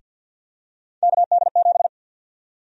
Morse code, heard as a keyed tone, about 46 wpm, 700 Hz. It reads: KD7
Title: USA Call Sign Prefixes - 50wpm